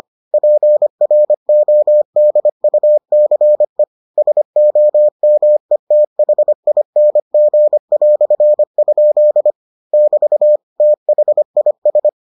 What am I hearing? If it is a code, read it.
Morse code, heard as a keyed tone, about 25 wpm, 600 Hz. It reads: PRODUCE SOMETHING"? <BT> THIS